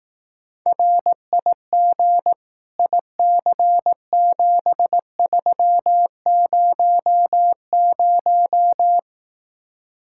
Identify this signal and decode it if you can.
Morse code, heard as a keyed tone, about 18 wpm, 700 Hz. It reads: RIG IC7300
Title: Speed-Racing Common QSO Phrases Encore - 15wpm